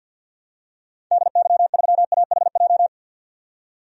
Morse code, heard as a keyed tone, about 52 wpm, 700 Hz. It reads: DY3AHJ
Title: DXCC-100 Call Signs - 6 Characters - 50wpm